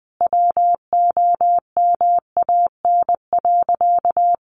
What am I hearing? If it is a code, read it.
Morse code, heard as a keyed tone, about 20 wpm, 700 Hz. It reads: WOMAN.
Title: Sentences from Top 1400 Words 20wpm